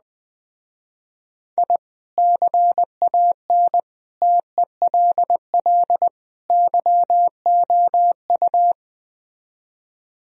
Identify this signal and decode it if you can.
Morse code, heard as a keyed tone, about 20 wpm, 700 Hz. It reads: I CAN TELL YOU
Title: Speed-Racing - Sets of 4 Words 17wpm